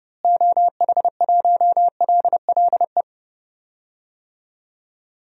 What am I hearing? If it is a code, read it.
Morse code, heard as a keyed tone, about 30 wpm, 700 Hz. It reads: OH1LLE
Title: DXCC-100 Call Signs - 6 Characters - Repeated 2x - 30wpm